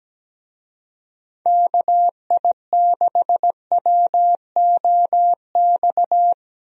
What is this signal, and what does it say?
Morse code, heard as a keyed tone, about 17 wpm, 700 Hz. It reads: KI6WOX
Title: USA Call Signs - 6 Characters - Repeated 2x - 17wpm